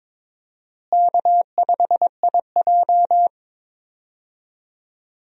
Morse code, heard as a keyed tone, about 22 wpm, 700 Hz. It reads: K5IJ
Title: USA Call Signs - 4 Characters - Repeated 2x - 22wpm